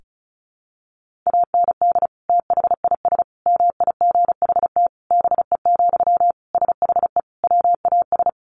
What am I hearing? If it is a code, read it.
Morse code, heard as a keyed tone, about 34 wpm, 700 Hz. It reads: AND THIS MIGHT BE, SHE WAS
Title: Sentences from Top 1500 Words 35wpm